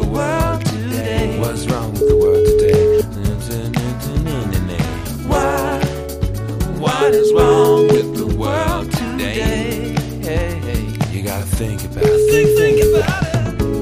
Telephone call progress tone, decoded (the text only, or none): ringback tone